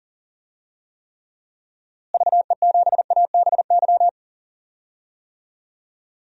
Morse code, heard as a keyed tone, about 38 wpm, 700 Hz. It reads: VE7ABY
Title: DXCC-100 Call Signs - 6 Characters - Repeated 2x - 40wpm